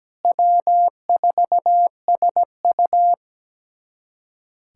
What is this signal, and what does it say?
Morse code, heard as a keyed tone, about 17 wpm, 700 Hz. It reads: W4SU
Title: USA Callsign DE Callsign - 4 Characters - 17wpm